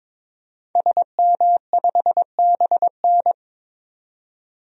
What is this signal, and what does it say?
Morse code, heard as a keyed tone, about 22 wpm, 700 Hz. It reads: SM5BN
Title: DXCC-100 Call Signs - 5 Characters - 22wpm